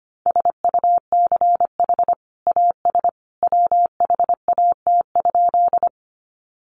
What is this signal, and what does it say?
Morse code, heard as a keyed tone, about 25 wpm, 700 Hz. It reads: SUCH AS WHAT?
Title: Sentences from Top 200 Words 25wpm